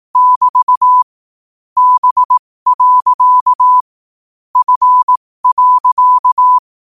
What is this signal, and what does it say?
Morse code, heard as a keyed tone, about 18 wpm, spaced as for 15 wpm, 1000 Hz. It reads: <BT> B. F.
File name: qotdfiles / 15wpm